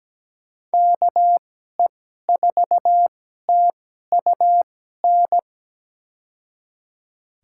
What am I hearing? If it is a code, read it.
Morse code, heard as a keyed tone, about 17 wpm, spaced as for 12 wpm, 700 Hz. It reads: KE4TUN
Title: USA Call Signs - 6 Characters - 2x Character Spacing - 17wpm